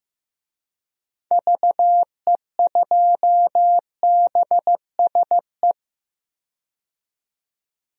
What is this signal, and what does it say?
Morse code, heard as a keyed tone, about 15 wpm, 700 Hz. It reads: VE2BSE
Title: DXCC-100 Call Signs - 6 Characters - 15wpm